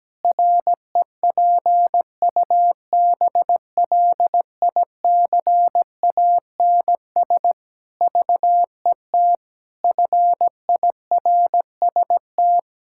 Morse code, heard as a keyed tone, about 17 wpm, 700 Hz. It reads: REPUBLICANS VET FIRST
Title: Pangram Challenge v2 - 17wpm